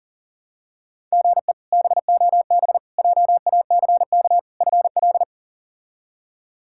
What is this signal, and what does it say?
Morse code, heard as a keyed tone, about 40 wpm, 700 Hz. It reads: GE BOB JACK FL
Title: K1USN SST QSO Practice - 40wpm